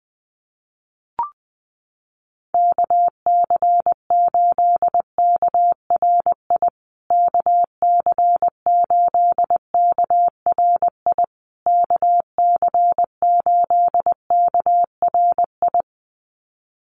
Morse code, heard as a keyed tone, about 20 wpm, 700 Hz. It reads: KC8KRI KC8KRI KC8KRI
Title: USA Call Signs Repeated 3x 20wpm